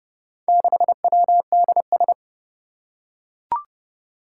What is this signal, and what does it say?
Morse code, heard as a keyed tone, about 28 wpm, 700 Hz. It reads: 6WDS
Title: Four Challenging Letter-Number Combinations 30wpm